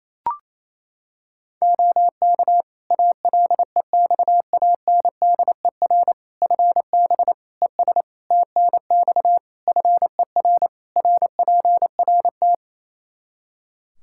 Morse code, heard as a keyed tone, about 28 wpm, 700 Hz. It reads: OK ALEXANDER FB ES TNX FER RPRT
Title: Practice QSO - Exchange 2 - Info - 28wpm